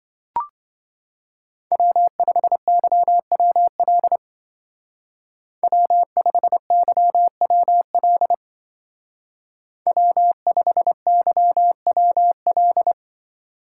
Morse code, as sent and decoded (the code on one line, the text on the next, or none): .-- ..... -.-- .-- .-.. / .-- ..... -.-- .-- .-.. / .-- ..... -.-- .-- .-..
W5YWL W5YWL W5YWL